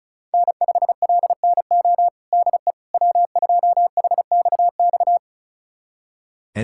Morse code, heard as a keyed tone, about 34 wpm, 700 Hz. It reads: N5LNO DE W2HXX
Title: USA Callsign DE Callsign - 5 Characters - 35wpm